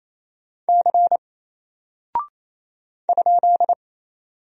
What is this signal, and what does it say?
Morse code, heard as a keyed tone, about 27 wpm, 700 Hz. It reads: C ?
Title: Single Character (Letter:Number:Punctuation:Prosign) 28wpm